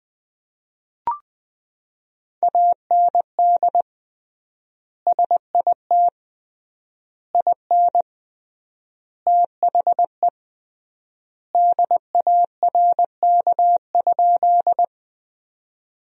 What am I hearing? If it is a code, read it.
Morse code, heard as a keyed tone, about 20 wpm, 700 Hz. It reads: AND SIT IN THE DARK?
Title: 3x Word-Spacing - Sentences from Top 500 Words - 20wpm